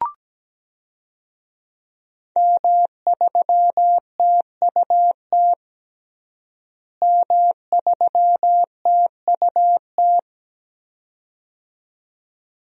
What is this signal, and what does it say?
Morse code, heard as a keyed tone, about 17 wpm, 700 Hz. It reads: M3TUT M3TUT
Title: DXCC-100 Call Signs - 5 Characters - Repeated 2x - 17wpm